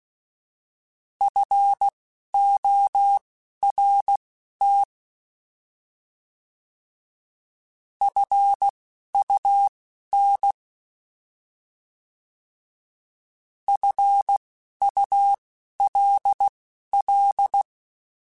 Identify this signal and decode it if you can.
Morse code, heard as a keyed tone, about 16 wpm, spaced as for 11 wpm, 800 Hz. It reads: FORT FUN FULL
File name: Alphabetical / Alphabetical mp3s / Fs @10